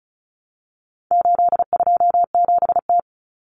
Morse code, heard as a keyed tone, about 34 wpm, 700 Hz. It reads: 827T
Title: Common Cut Numbers - 4-Digit 35wpm